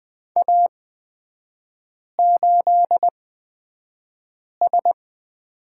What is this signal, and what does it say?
Morse code, heard as a keyed tone, about 20 wpm, spaced as for 5 wpm, 700 Hz. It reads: A8S
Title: 243 Lesson 39e - Letter z - Three Characters 20wpm